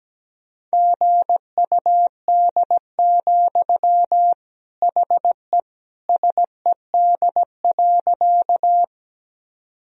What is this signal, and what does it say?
Morse code, heard as a keyed tone, about 17 wpm, 700 Hz. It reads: GUD, HE SED.
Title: Sentences from Top 200 Words and Abbreviations 17wpm